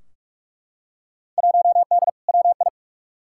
Morse code, heard as a keyed tone, about 44 wpm, 700 Hz. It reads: 1D WI
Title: Field Day Exchanges - 45wpm